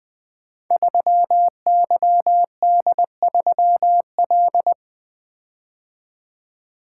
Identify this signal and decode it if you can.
Morse code, heard as a keyed tone, about 20 wpm, 700 Hz. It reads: 3YD3L
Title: Code Groups - 5 Letter:Numbers - 20wpm